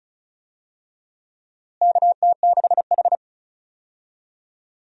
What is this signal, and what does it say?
Morse code, heard as a keyed tone, about 36 wpm, 700 Hz. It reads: KT6H
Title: USA Call Signs - 4 Characters - Repeated 2x - 35wpm